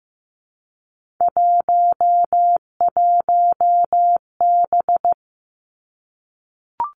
Morse code, heard as a keyed tone, about 15 wpm, 700 Hz. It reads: 11B
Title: Three Challenging Letter-Number Combinations 15wpm